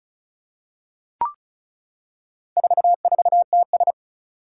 Morse code, heard as a keyed tone, about 36 wpm, 700 Hz. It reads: 44TS
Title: Four Challenging Letter-Number Combinations 35wpm